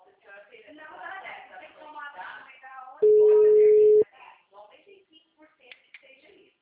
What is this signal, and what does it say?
Ringback tone (Europe, 425 Hz), with speech.
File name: out- -2044- - - .24671